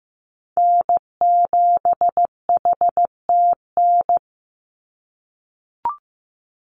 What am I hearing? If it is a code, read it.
Morse code, heard as a keyed tone, about 15 wpm, 700 Hz. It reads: N7HTN
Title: USA Call Signs 15wpm